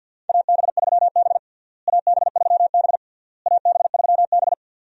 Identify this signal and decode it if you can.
Morse code, heard as a keyed tone, about 52 wpm, 700 Hz. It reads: AB3B AB3B AB3B
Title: USA Call Signs - 4 Characters - Repeated 3x - 50wpm